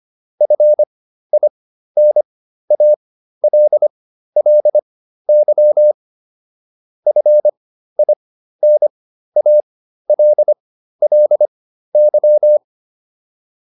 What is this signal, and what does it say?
Morse code, heard as a keyed tone, about 25 wpm, spaced as for 13 wpm, 600 Hz. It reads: FINALLY FINALLY
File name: LY7 13